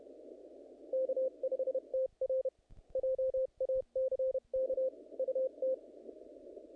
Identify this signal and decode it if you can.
Morse code, heard as a keyed tone, about 29 wpm, 530 Hz. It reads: K5TR JACK UT
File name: contests / naqp-cw-k5tr-op-k5pi- -2000